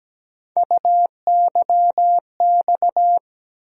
Morse code, heard as a keyed tone, about 17 wpm, 700 Hz. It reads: UYX
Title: USA Call Sign Postfixes - 3 Letters 17wpm